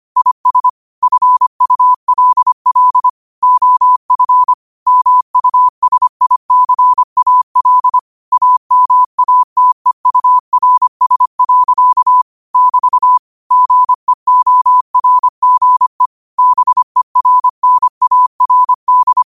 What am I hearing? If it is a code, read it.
Morse code, heard as a keyed tone, about 25 wpm, 1000 Hz. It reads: IS FULL OF MUSICAL AMATEURS. <BT> GEORGE BERNARD
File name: qotdfiles / 25wpm